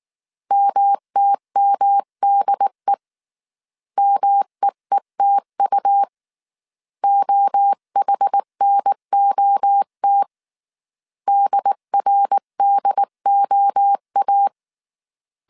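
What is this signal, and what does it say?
Morse code, heard as a keyed tone, about 19 wpm, 800 Hz. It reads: MTMDE MEETU OHNOT DRDOA
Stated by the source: Morse code dictation at 18 wpm from the IW3SGT QRS Morse code training course, five-character groups of 300 characters.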